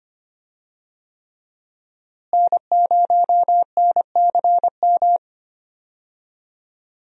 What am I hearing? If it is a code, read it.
Morse code, heard as a keyed tone, about 25 wpm, 700 Hz. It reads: N0NCM
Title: USA Call Signs - 5 Characters - Repeated 3x - 25wpm